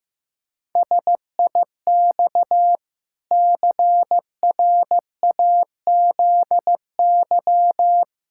Morse code, heard as a keyed tone, about 15 wpm, 700 Hz. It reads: SIX CRAZY
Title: Pangram Challenge v2 - 15wpm